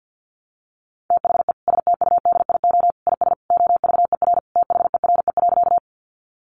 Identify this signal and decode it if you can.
Morse code, heard as a keyed tone, about 50 wpm, 700 Hz. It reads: THE STUDIO IS OVER THERE.
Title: Sentences from Top 1200 Words 50wpm